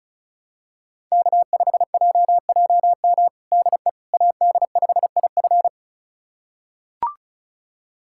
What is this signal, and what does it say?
Morse code, heard as a keyed tone, about 36 wpm, 700 Hz. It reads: K5JJM DE AD5IF
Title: USA Callsign DE Callsign - 5 Characters - 35wpm